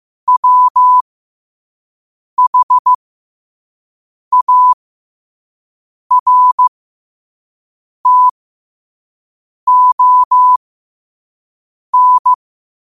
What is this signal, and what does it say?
Morse code, heard as a keyed tone, about 15 wpm, spaced as for 5 wpm, 1000 Hz. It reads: WHARTON
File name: qotdfiles / 5wpm